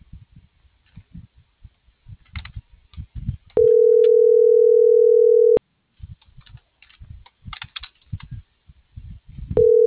Ringback tone (North America), 440 + 480 Hz.